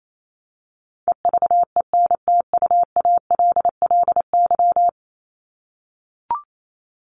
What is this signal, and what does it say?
Morse code, heard as a keyed tone, about 28 wpm, 700 Hz. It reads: EVENTUALLY
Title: LY Suffix - Top 100 Words with Base Words 28wpm